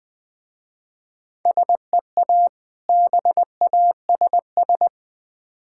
Morse code, heard as a keyed tone, about 20 wpm, 700 Hz. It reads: SEA BASS
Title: Difficult Words with Hints - 20wpm